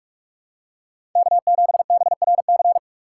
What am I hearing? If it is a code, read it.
Morse code, heard as a keyed tone, about 46 wpm, 690 Hz. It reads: K7BRC
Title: USA Call Signs - 5 Characters - 4x Character Spacing - 45wpm